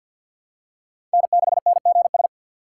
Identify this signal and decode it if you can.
Morse code, heard as a keyed tone, about 52 wpm, 700 Hz. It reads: N6NGS
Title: USA Call Signs - 5 Characters - 4x Character Spacing - 50wpm